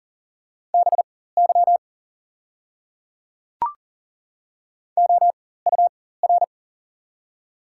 Morse code, heard as a keyed tone, about 40 wpm, spaced as for 19 wpm, 700 Hz. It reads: BY OUR
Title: 4x Character Spacing - Top 200 Words - 40wpm